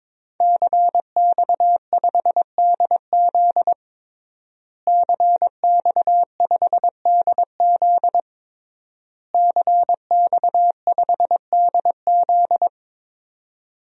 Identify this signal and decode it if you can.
Morse code, heard as a keyed tone, about 22 wpm, 700 Hz. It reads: CX5DZ CX5DZ CX5DZ
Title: DXCC-100 Call Signs - 5 Characters - Repeated 3x - 22wpm